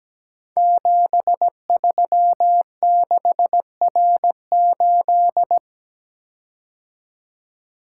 Morse code, heard as a keyed tone, about 17 wpm, 700 Hz. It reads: 736R8
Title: Code Groups - 5 Letter:Numbers - 17wpm